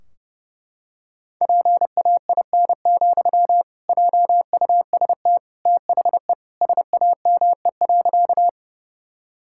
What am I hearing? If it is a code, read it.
Morse code, heard as a keyed tone, about 28 wpm, 700 Hz. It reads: PAIN, JUST THE SAME.
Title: Sentences from Top 1600 Words 30wpm